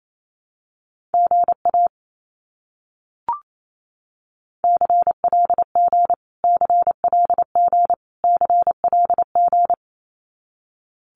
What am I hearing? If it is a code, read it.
Morse code, heard as a keyed tone, about 28 wpm, 700 Hz. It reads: GA CLG CLG CLG
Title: Abbreviations Repeated 3x - 28wpm